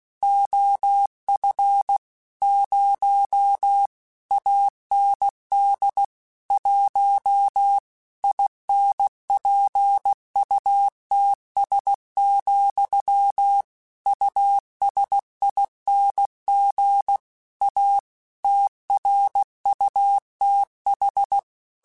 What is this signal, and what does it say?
Morse code, heard as a keyed tone, about 16 wpm, 800 Hz. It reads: OF 0 AND 1 INPUTS, USING A TRUTH
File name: Technical Subjects / Technical Subjects-mp3s / Gates Intro @15